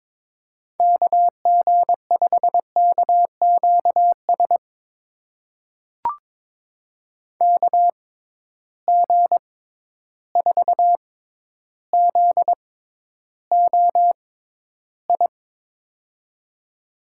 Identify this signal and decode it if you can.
Morse code, heard as a keyed tone, about 22 wpm, spaced as for 11 wpm, 700 Hz. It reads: KG5KQS KG4ZOI
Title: USA Call Signs - 6 Characters - 6x Character Spacing - 22wpm